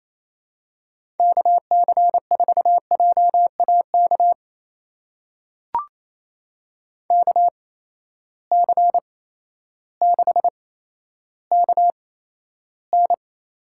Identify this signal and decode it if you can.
Morse code, heard as a keyed tone, about 28 wpm, spaced as for 13 wpm, 700 Hz. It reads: KC4JAK KC6KN
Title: USA Call Signs - 6 Characters - 8x Character Spacing - 28wpm